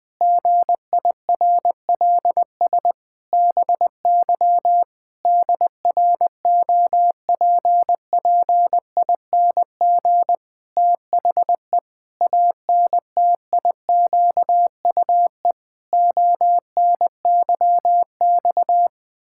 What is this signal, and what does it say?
Morse code, heard as a keyed tone, about 20 wpm, 700 Hz. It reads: GIRLS BY DROPPING THE ANTIQUE ONYX
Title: Pangram Challenge v2 - 20wpm